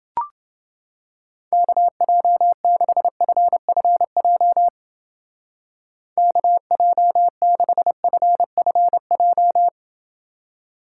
Morse code, heard as a keyed tone, about 28 wpm, 700 Hz. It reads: KJ6FFJ KJ6FFJ